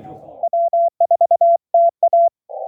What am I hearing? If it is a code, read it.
Morse code, heard as a keyed tone, about 23 wpm, 690 Hz. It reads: W4TA